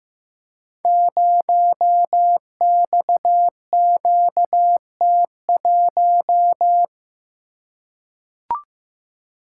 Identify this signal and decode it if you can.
Morse code, heard as a keyed tone, about 15 wpm, 700 Hz. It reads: 0XQT1
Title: Code Groups - 5 Letter:Numbers - 15wpm